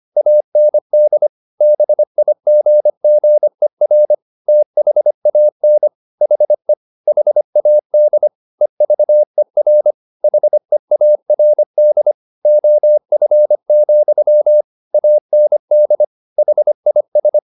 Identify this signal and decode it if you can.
Morse code, heard as a keyed tone, about 25 wpm, 600 Hz. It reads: AND BIGGER THAN HE HAD EVER HEARD OF, AND HIS
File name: the old man and the sea part 1 25wpm